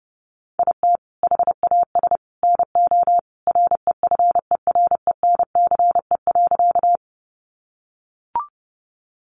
Morse code, heard as a keyed tone, about 30 wpm, 700 Hz. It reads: IT HAS NO REFERENCE.
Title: Sentences from Top 1200 Words 30wpm